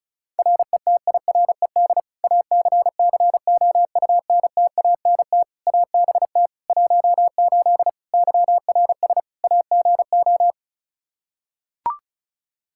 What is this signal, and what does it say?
Morse code, heard as a keyed tone, about 34 wpm, 700 Hz. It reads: RETIRED ACCOUNTANT ABT 18 YRS AGO
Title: K1USN SST QSO Practice 35wpm